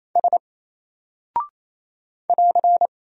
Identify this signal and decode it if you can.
Morse code, heard as a keyed tone, about 27 wpm, 700 Hz. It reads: S <AR>
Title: Single Character (Letter:Number:Punctuation:Prosign) 28wpm